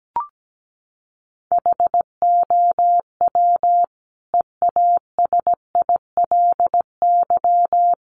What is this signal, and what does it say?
Morse code, heard as a keyed tone, about 17 wpm, 700 Hz. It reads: HOW EASILY
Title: Sentences from Top 900 Words 17wpm